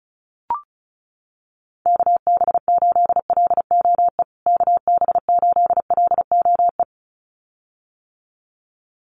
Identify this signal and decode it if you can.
Morse code, heard as a keyed tone, about 34 wpm, 700 Hz. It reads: KB8LOE KB8LOE
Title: USA Call Signs Repeated 2x 35wpm